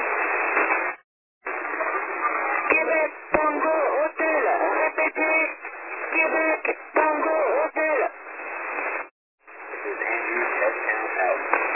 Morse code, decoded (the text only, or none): none